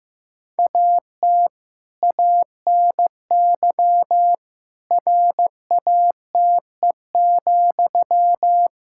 Morse code, heard as a keyed tone, about 15 wpm, 700 Hz. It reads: AT ANY RATE,
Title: Sentences from Top 1000 Words - Review Part 6 - 15wpm